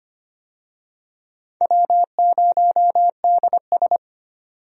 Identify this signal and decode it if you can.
Morse code, heard as a keyed tone, about 25 wpm, 700 Hz. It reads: W0DS
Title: USA Call Signs - 4 Characters - 25wpm